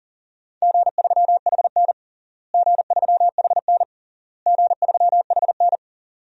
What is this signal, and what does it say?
Morse code, heard as a keyed tone, about 40 wpm, 700 Hz. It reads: G3HN G3HN G3HN
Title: DXCC-100 Call Signs - 4 Characters - Repeated 3x - 40wpm